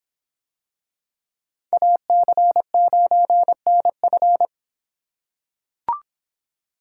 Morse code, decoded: AC9NF

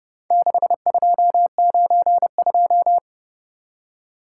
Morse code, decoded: 6292

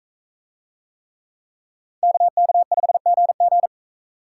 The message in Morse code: KK5GG

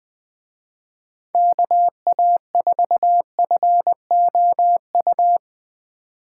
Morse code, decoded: KA4FOU

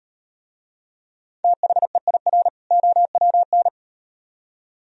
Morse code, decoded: THEIR OWN

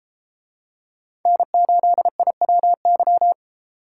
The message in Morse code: N8IWY

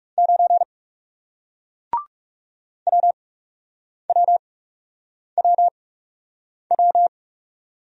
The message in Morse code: 9 WWWW

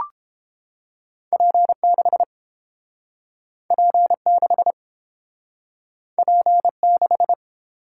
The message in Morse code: P6 P6 P6

